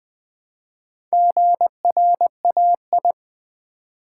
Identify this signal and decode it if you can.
Morse code, heard as a keyed tone, about 20 wpm, 700 Hz. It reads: GRAI
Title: Build A Word - 1 Letter At A Time - 3 to 5-Letter Words - 20wpm